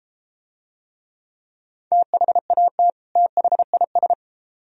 Morse code, heard as a keyed tone, about 33 wpm, 700 Hz. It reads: THAT THIS